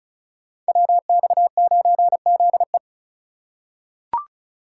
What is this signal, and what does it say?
Morse code, heard as a keyed tone, about 35 wpm, 700 Hz. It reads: WX9ZE
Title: Code Groups - 5 Letter:Numbers - 35wpm